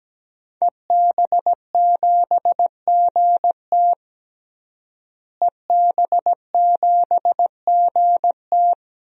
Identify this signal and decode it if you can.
Morse code, heard as a keyed tone, about 17 wpm, 700 Hz. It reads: EB7GT EB7GT
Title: DXCC-100 Call Signs - 5 Characters - Repeated 2x - 17wpm